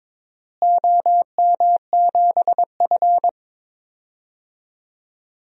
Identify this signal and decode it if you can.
Morse code, heard as a keyed tone, about 21 wpm, 700 Hz. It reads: OM7F
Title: DXCC-100 Call Signs - 4 Characters - Repeated 2x - 22wpm